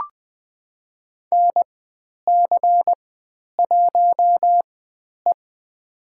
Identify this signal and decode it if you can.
Morse code, heard as a keyed tone, about 20 wpm, spaced as for 10 wpm, 700 Hz. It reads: NC1E
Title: 095 Lesson 16g - Letter c - Call Signs 20wpm